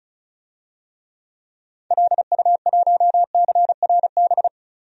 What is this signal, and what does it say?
Morse code, heard as a keyed tone, about 34 wpm, 700 Hz. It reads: LU1CRB
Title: DXCC-100 Call Signs - 6 Characters - Repeated 2x - 35wpm